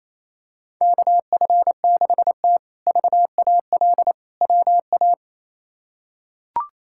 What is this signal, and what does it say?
Morse code, heard as a keyed tone, about 28 wpm, 700 Hz. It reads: KF6T VAL WA
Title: K1USN SST QSO Practice - 28wpm